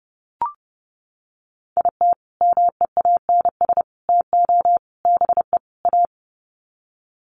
Morse code, heard as a keyed tone, about 30 wpm, 700 Hz. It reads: IT MEANS TO BE A
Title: Sets of 5 Words Encore2 30wpm